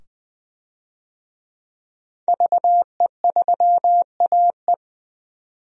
Morse code, heard as a keyed tone, about 20 wpm, 700 Hz. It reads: VE3AE
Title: DXCC-100 Call Signs - 5 Characters - 20wpm